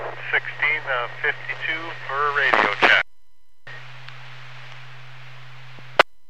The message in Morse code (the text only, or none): none